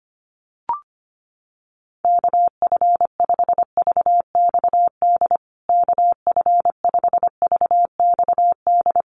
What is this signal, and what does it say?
Morse code, heard as a keyed tone, about 25 wpm, 700 Hz. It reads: KF5VXD KF5VXD
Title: USA Call Signs Repeated 3x 25wpm